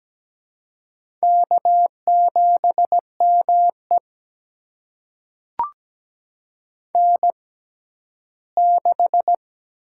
Morse code, heard as a keyed tone, about 17 wpm, spaced as for 12 wpm, 700 Hz. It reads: K7ME N6
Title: USA Call Signs - 4 Characters - 6x Character Spacing - 17wpm